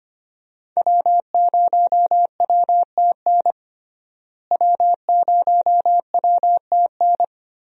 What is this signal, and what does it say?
Morse code, heard as a keyed tone, about 25 wpm, 700 Hz. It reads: W0WTN W0WTN
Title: USA Call Signs - 5 Characters - Repeated 3x - 25wpm